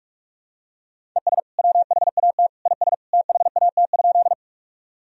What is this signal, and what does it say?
Morse code, heard as a keyed tone, about 47 wpm, 700 Hz. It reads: ES WHAT IS THAT?
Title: Sentences from Top 100 Words and Abbreviations 45wpm